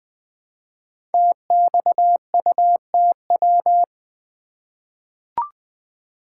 Morse code, heard as a keyed tone, about 20 wpm, 700 Hz. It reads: TXUTW
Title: Code Groups - 5 Letters - 20wpm